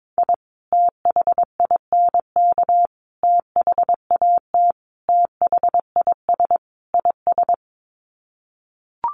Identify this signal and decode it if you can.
Morse code, heard as a keyed tone, about 22 wpm, 700 Hz. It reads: I THINK THAT THIS IS
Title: Sets of 5 Words Encore2 22wpm